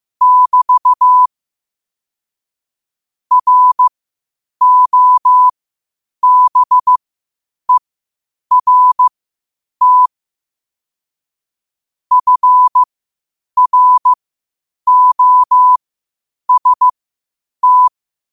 Morse code, heard as a keyed tone, about 15 wpm, spaced as for 8 wpm, 1000 Hz. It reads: <BT> ROBERT FROST
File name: qotdfiles / 8wpm